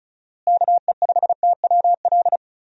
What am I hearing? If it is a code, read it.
Morse code, heard as a keyed tone, about 34 wpm, 700 Hz. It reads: KE5TWL
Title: USA Call Signs - 6 Characters - Repeated 3x - 35wpm